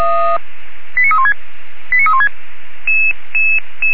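Five-tone selective call (ZVEI), heard as quick runs of five tones, tones of about 70 ms. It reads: 89316 89316
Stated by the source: ZVEI five-tone call received by SDR.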